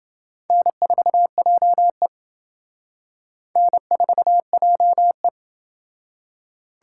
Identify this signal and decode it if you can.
Morse code, heard as a keyed tone, about 28 wpm, 700 Hz. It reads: N4JE N4JE